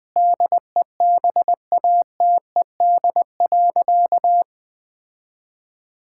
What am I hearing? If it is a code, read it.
Morse code, heard as a keyed tone, about 20 wpm, 700 Hz. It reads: DEBATED.
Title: Sentences with 7-Word Phrases - 20wpm